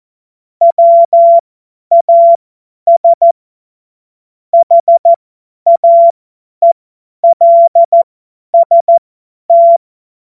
Morse code, heard as a keyed tone, about 14 wpm, spaced as for 10 wpm, 680 Hz. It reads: WAS HAELST
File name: Morsekurs / Witze / Witz 14. März 2023 50 F70